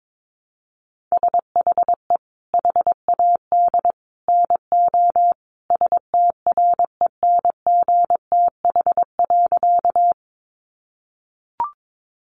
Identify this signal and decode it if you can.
Morse code, heard as a keyed tone, about 22 wpm, 700 Hz. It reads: SHE HAD NO STRENGTH.
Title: Sentences from Top 1300 Words 22wpm